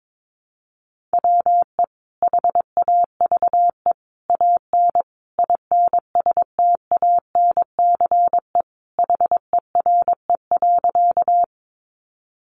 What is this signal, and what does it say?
Morse code, heard as a keyed tone, about 22 wpm, 700 Hz. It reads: WE HAVE AN INSTANCE HERE.
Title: Sentences from Top 800 Words 22wpm